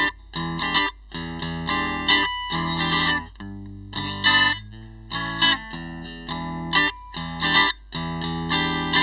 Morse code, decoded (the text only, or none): none